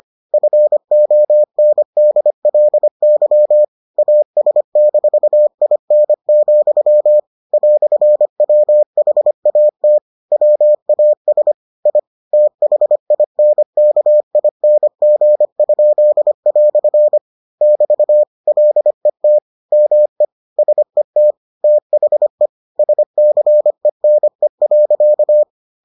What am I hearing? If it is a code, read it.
Morse code, heard as a keyed tone, about 25 wpm, 600 Hz. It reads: FONDLY AS-IN, "WHAT WAS I THINKING?" <BT> LET ME SET THE SCENE.